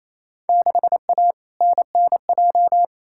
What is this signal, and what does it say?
Morse code, heard as a keyed tone, about 28 wpm, 700 Hz. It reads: 6A NNJ
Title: Field Day Exchanges Encore - 28wpm